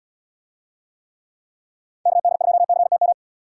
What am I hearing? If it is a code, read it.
Morse code, heard as a keyed tone, about 70 wpm, 700 Hz. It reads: KD2CTM